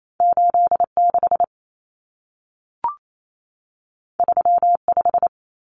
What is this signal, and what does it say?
Morse code, heard as a keyed tone, about 28 wpm, 700 Hz. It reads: 86 35
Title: Common Cut Numbers - 2-Digit 28wpm